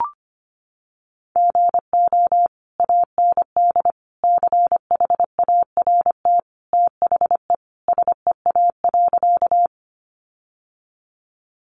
Morse code, heard as a keyed tone, about 25 wpm, 700 Hz. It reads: GO AND CHART THE SEA.
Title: Sentences from Top 1300 Words 25wpm